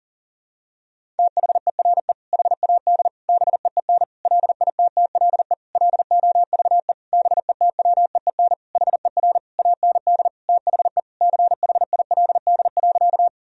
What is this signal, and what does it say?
Morse code, heard as a keyed tone, about 37 wpm, 700 Hz. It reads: THERE HAD BEEN LITTLE LOVE BETWEEN HER AND THE CHILD.
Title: Sentences from Top 1000 Words - Review Part 1 - 40wpm